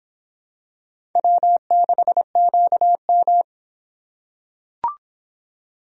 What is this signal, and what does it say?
Morse code, heard as a keyed tone, about 25 wpm, 700 Hz. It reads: W6QM